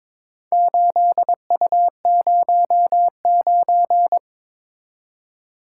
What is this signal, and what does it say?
Morse code, heard as a keyed tone, about 22 wpm, 700 Hz. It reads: 8U09
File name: Four Letter-Number Combinations - 22wpm